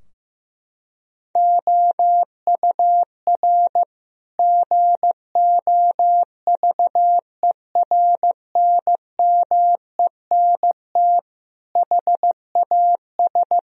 Morse code, heard as a keyed tone, about 16 wpm, 700 Hz. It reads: OUR GOVERNMENT HAS
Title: Sentences from Top 1000 Words - Review Part 4 - 15wpm